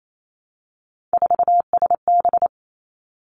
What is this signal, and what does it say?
Morse code, heard as a keyed tone, about 28 wpm, 700 Hz. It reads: 4SB
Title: Three Challenging Letter-Number Combinations 28wpm